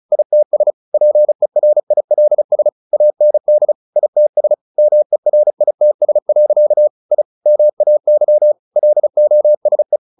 Morse code, heard as a keyed tone, about 34 wpm, 600 Hz. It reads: ITS PERILS AND ITS MERITS. I MAY LOSE
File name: the old man and the sea 2 35wpm